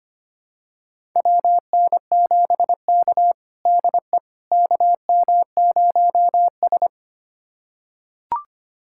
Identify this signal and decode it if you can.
Morse code, heard as a keyed tone, about 25 wpm, 700 Hz. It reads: WN7K DE KM0S
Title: USA Callsign DE Callsign - 4 Characters - 25wpm